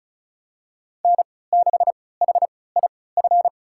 Morse code, heard as a keyed tone, about 36 wpm, spaced as for 20 wpm, 700 Hz. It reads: N6HIF
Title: USA Call Signs - 5 Characters - 3x Character Spacing - 35wpm